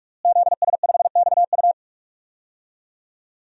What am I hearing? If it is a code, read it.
Morse code, heard as a keyed tone, about 46 wpm, 700 Hz. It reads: ZS5XU
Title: DXCC-100 Call Signs - 5 Characters - Repeated 3x - 45wpm